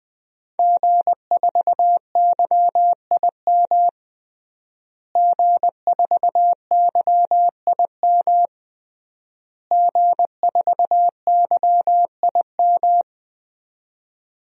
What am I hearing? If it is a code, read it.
Morse code, heard as a keyed tone, about 20 wpm, 700 Hz. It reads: G4YIM G4YIM G4YIM
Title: DXCC-100 Call Signs - 5 Characters - Repeated 3x - 20wpm